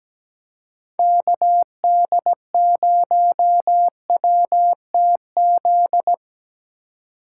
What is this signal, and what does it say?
Morse code, heard as a keyed tone, about 17 wpm, 700 Hz. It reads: KD0WTZ
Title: USA Call Signs - 6 Characters - 3x Character Spacing - 17wpm